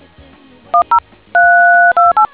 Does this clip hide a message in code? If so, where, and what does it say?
Yes: DTMF from 0.7 to 2.3 s, heard as telephone touch tones, tones of about 160 ms. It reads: 10328